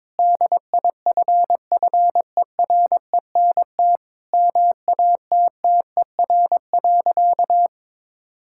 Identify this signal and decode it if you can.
Morse code, heard as a keyed tone, about 21 wpm, 700 Hz. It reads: DIFFERENT MATTER.
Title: Sentences from Top 1600 Words 22wpm